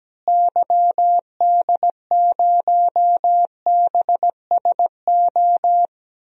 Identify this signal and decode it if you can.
Morse code, heard as a keyed tone, about 17 wpm, 700 Hz. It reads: YD0BSO
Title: DXCC-100 Call Signs - 6 Characters - Repeated 3x - 17wpm